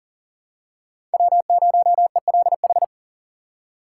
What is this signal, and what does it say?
Morse code, heard as a keyed tone, about 40 wpm, 700 Hz. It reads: W0ELH
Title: USA Call Signs - 5 Characters - 2x Character Spacing - 40wpm